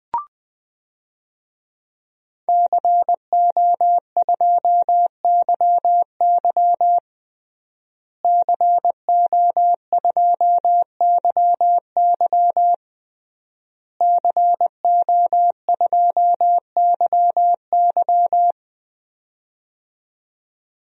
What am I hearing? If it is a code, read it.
Morse code, heard as a keyed tone, about 20 wpm, 700 Hz. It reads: CO2YY CO2YY CO2YY
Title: DXCC-100 Call Signs - 5 Characters - Repeated 3x - 20wpm